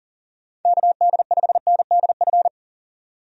Morse code, heard as a keyed tone, about 37 wpm, 700 Hz. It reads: KD5NDF
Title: USA Call Signs - 6 Characters - 40wpm